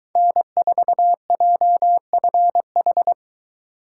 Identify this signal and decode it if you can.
Morse code, heard as a keyed tone, about 23 wpm, 700 Hz. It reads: N4JFH